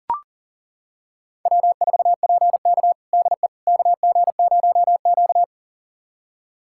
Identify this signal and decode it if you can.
Morse code, heard as a keyed tone, about 40 wpm, 700 Hz. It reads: W4PK DE KG0Q
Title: USA Callsign DE Callsign - 4 Characters - 40wpm